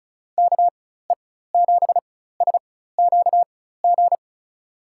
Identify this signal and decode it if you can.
Morse code, heard as a keyed tone, about 36 wpm, spaced as for 16 wpm, 700 Hz. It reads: KE7SQG
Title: USA Call Signs - 6 Characters - 4x Character Spacing - 35wpm